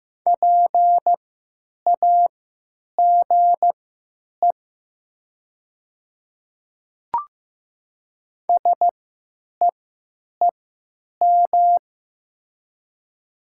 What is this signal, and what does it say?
Morse code, heard as a keyed tone, about 15 wpm, spaced as for 9 wpm, 700 Hz. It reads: PAGE SEEM
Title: 3x Character Spacing - Top 300 Words - 15wpm